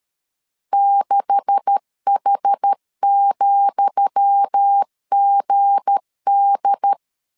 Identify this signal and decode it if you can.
Morse code, heard as a keyed tone, about 13 wpm, 800 Hz. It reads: 6H,GD
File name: LEZIONI / 10 SIGNS / 10D COMA 12wpm 2544